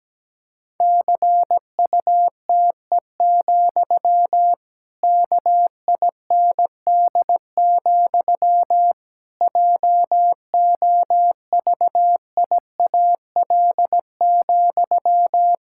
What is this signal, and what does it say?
Morse code, heard as a keyed tone, about 17 wpm, 700 Hz. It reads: CUTE, KIND, JOVIAL,
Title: Pangram Challenge v2 - 17wpm